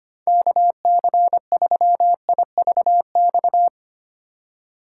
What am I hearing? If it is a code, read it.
Morse code, heard as a keyed tone, about 25 wpm, 700 Hz. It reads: KC3IVX
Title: USA Call Signs - 6 Characters - 8x Character Spacing - 25wpm